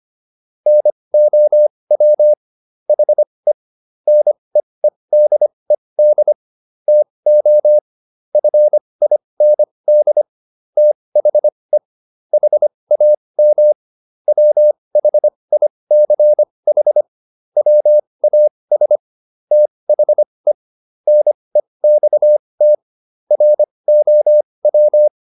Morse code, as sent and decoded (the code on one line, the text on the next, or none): -. --- .-- / .... / -. . -.. -.. / - --- / ..-. .. -. -.. / - .... / .... .- -- / .-- .... .. -.-. .... / .-- .- ... / - .... / -. -..- - / .-. --- .--
NOW HE NEEDED TO FIND THE HAM WHICH WAS THE NEXT ROW